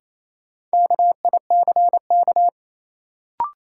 Morse code, heard as a keyed tone, about 27 wpm, 700 Hz. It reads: KICK
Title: Sentences from Top 1300 Words - New Word Drill - 28wpm